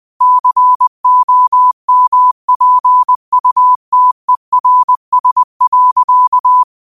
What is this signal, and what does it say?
Morse code, heard as a keyed tone, about 20 wpm, 1000 Hz. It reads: COMPUTERS.
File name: qotdfiles / 20wpm